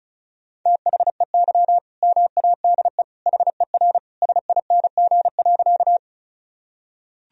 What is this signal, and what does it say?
Morse code, heard as a keyed tone, about 35 wpm, 700 Hz. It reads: THEY MADE HER SING.
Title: Sentences from Top 1000 Words - Review Part 2 - 35wpm